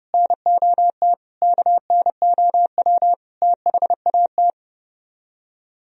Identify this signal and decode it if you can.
Morse code, heard as a keyed tone, about 40 wpm, 700 Hz. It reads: NOT KNOW THAT